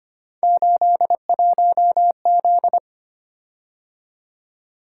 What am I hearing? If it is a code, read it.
Morse code, heard as a keyed tone, about 25 wpm, 700 Hz. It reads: 81Z
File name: Three Letter-Number Combinations - 25wpm